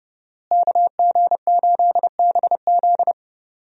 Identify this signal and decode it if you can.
Morse code, heard as a keyed tone, about 31 wpm, 700 Hz. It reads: KG8BZ
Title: USA Call Signs - 5 Characters - Repeated 2x - 30wpm